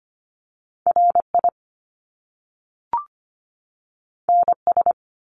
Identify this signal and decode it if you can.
Morse code, heard as a keyed tone, about 25 wpm, 700 Hz. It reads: RI NS
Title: Top Two Letter Combinations in Words 25wpm